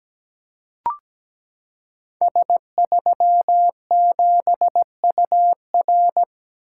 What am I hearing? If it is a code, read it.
Morse code, heard as a keyed tone, about 17 wpm, 700 Hz. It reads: S37UR
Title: Code Groups - 5 Letter:Numbers - 17wpm